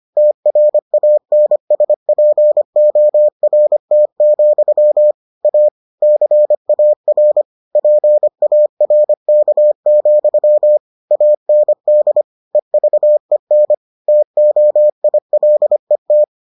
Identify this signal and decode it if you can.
Morse code, heard as a keyed tone, about 25 wpm, 600 Hz. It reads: TRANSPORT, A CAR PARK, AND EVEN TOILET